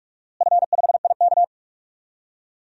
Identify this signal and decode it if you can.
Morse code, heard as a keyed tone, about 46 wpm, 700 Hz. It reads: F5IK
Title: DXCC-100 Call Signs - 4 Characters - Repeated 2x - 45wpm